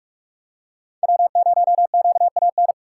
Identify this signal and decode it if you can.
Morse code, heard as a keyed tone, about 44 wpm, 700 Hz. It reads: W0QAN